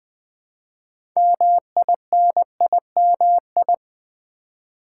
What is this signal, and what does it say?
Morse code, heard as a keyed tone, about 20 wpm, 700 Hz. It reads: MINIMI
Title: Build A Word - 2 Letters At A Time - 8-Letter Words - 20wpm